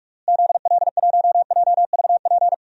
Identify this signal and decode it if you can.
Morse code, heard as a keyed tone, about 41 wpm, 700 Hz. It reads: ZL1JVP
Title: DXCC-100 Call Signs - 6 Characters - Repeated 3x - 45wpm